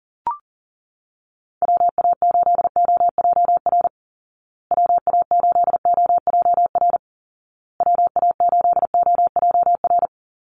Morse code, heard as a keyed tone, about 40 wpm, 700 Hz. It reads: WA8OJR WA8OJR WA8OJR
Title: Call Signs Repeated 2x - 40wpm